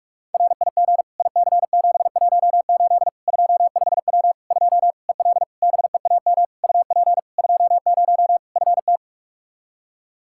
Morse code, heard as a keyed tone, about 41 wpm, 700 Hz. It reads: RIG IC718 25W 2 EL BEAM UP 20 FT
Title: Practice QSO - Exchange 2 - Info - 45wpm